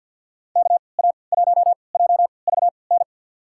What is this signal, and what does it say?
Morse code, heard as a keyed tone, about 50 wpm, spaced as for 28 wpm, 700 Hz. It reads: KA1JVN
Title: USA Call Signs - 6 Characters - 3x Character Spacing - 50wpm